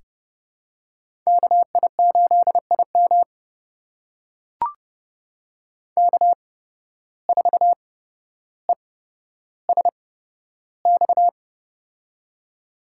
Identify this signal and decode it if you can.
Morse code, heard as a keyed tone, about 30 wpm, spaced as for 13 wpm, 700 Hz. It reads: KI8IM K4ESX
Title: USA Call Signs - 5 Characters - 8x Character Spacing - 30wpm